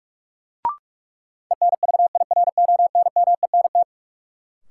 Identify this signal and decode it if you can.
Morse code, heard as a keyed tone, about 46 wpm, 700 Hz. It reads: ENVIRONMENT
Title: Sentences from Top 1000 Words - Review Part 5 - New Word Drill - 45wpm